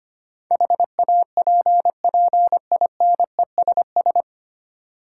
Morse code, heard as a keyed tone, about 25 wpm, 700 Hz. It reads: HAPPINESS
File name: practice / 25wpm / 9 Letter Words - First 3 Letters Given - 25wpm